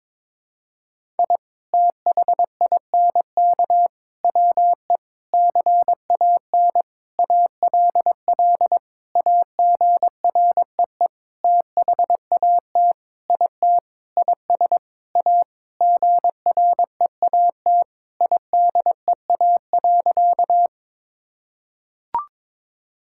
Morse code, heard as a keyed tone, about 22 wpm, 700 Hz. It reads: I THINK WE CAN ALL AGREE THAT IT IS A GREAT IDEA.
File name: Sentences with 7-Word Phrases - 22wpm